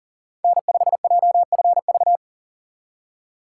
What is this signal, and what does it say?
Morse code, heard as a keyed tone, about 38 wpm, 700 Hz. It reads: N5JFV
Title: USA Call Signs - 5 Characters - 4x Character Spacing - 40wpm